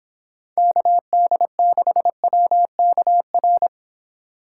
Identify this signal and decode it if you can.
Morse code, heard as a keyed tone, about 29 wpm, 700 Hz. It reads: KD6WKR